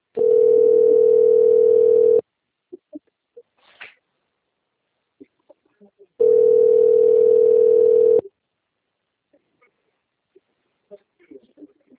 Ringback tone (North America), 440 + 480 Hz.